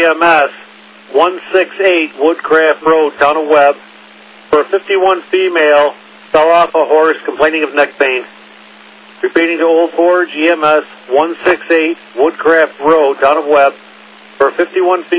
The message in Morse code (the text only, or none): none